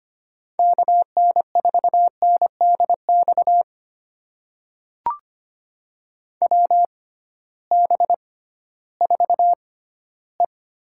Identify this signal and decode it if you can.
Morse code, heard as a keyed tone, about 25 wpm, spaced as for 14 wpm, 700 Hz. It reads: KN4NDX WB4E
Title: USA Call Signs - 6 Characters - 6x Character Spacing - 25wpm